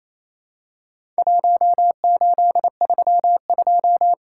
Morse code, heard as a keyed tone, about 28 wpm, 700 Hz. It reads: 1832
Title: Worked All Europe DX Contest - QTC Exchange - 28wpm